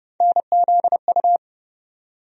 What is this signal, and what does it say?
Morse code, heard as a keyed tone, about 30 wpm, 700 Hz. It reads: NZU
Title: USA Call Sign Postfixes - 3 Letters 30wpm